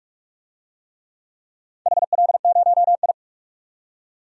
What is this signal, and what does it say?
Morse code, heard as a keyed tone, about 44 wpm, 700 Hz. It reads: HL0I
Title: DXCC-100 Call Signs - 4 Characters - 45wpm